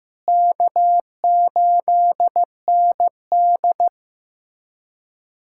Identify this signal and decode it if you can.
Morse code, heard as a keyed tone, about 15 wpm, 700 Hz. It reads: K8ND
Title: USA Callsign DE Callsign - 4 Characters - 15wpm